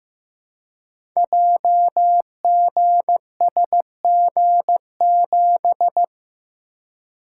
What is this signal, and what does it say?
Morse code, heard as a keyed tone, about 15 wpm, 700 Hz. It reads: JGSG7
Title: Code Groups - 5 Letter:Numbers - 15wpm